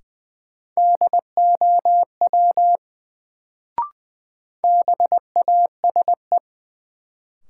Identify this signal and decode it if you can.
Morse code, heard as a keyed tone, about 20 wpm, 700 Hz. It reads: DOW BASE
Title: Sets of 5 Words - New Word Drill - 20wpm